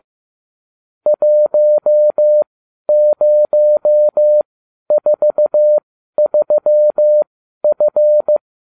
Morse code, heard as a keyed tone, about 15 wpm, spaced as for 11 wpm, 600 Hz. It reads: 1043F